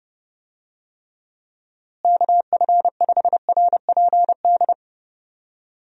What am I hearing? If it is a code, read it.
Morse code, heard as a keyed tone, about 30 wpm, 700 Hz. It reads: KF5RPD
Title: USA Call Signs - 6 Characters - Repeated 3x - 30wpm